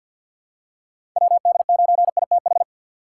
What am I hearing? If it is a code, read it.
Morse code, heard as a keyed tone, about 52 wpm, 700 Hz. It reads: WD9ITH
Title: USA Call Signs - 6 Characters - 4x Character Spacing - 50wpm